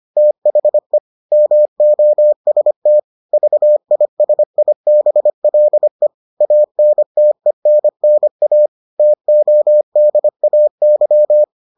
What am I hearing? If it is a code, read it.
Morse code, heard as a keyed tone, about 25 wpm, 600 Hz. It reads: THE MOST VISIBLE ANTENNA TODAY